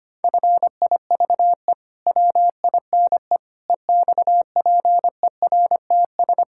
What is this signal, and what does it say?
Morse code, heard as a keyed tone, about 25 wpm, 700 Hz. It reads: FIVE WINE EXPERTS
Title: Pangram Challenge v2 - 25wpm